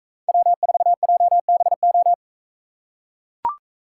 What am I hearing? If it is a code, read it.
Morse code, heard as a keyed tone, about 42 wpm, 700 Hz. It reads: W4JBO